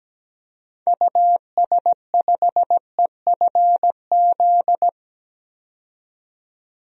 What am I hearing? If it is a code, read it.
Morse code, heard as a keyed tone, about 17 wpm, 700 Hz. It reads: US5EFZ
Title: DXCC-100 Call Signs - 6 Characters - 17wpm